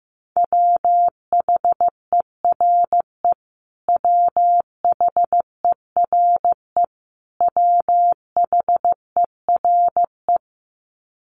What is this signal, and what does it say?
Morse code, heard as a keyed tone, about 15 wpm, 700 Hz. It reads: WHERE WHERE WHERE
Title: Top 200 Words Repeated 15wpm